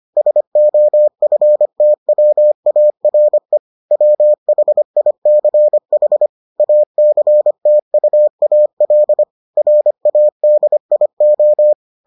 Morse code, heard as a keyed tone, about 25 wpm, 600 Hz. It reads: SOFTWARE WHICH ACTUAL RADIO